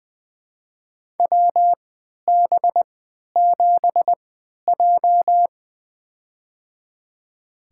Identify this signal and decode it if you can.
Morse code, heard as a keyed tone, about 20 wpm, spaced as for 11 wpm, 700 Hz. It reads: WB7J
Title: USA Call Signs - 4 Characters - 3x Character Spacing - 20wpm